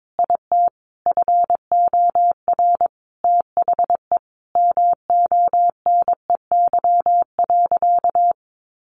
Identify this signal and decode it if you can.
Morse code, heard as a keyed tone, about 22 wpm, 700 Hz. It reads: IT FOR THE MONEY.
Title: Sentences from Top 1300 Words 22wpm